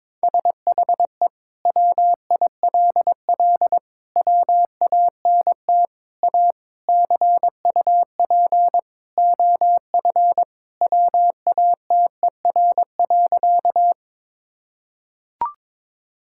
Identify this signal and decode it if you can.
Morse code, heard as a keyed tone, about 21 wpm, 700 Hz. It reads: SHE WILL WANT A CUP OF WATER.
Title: Sentences from Top 1600 Words 22wpm